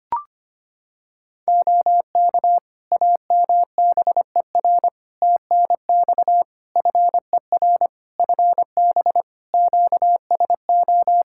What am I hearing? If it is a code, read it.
Morse code, heard as a keyed tone, about 25 wpm, 700 Hz. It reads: OK AMBER TNX FER FB QSO
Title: Practice QSO - Exchange 4 - Ending - 25wpm